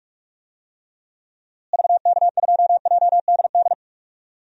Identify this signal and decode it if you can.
Morse code, heard as a keyed tone, about 47 wpm, 700 Hz. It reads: VK2JDD
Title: DXCC-100 Call Signs - 6 Characters - Repeated 2x - 45wpm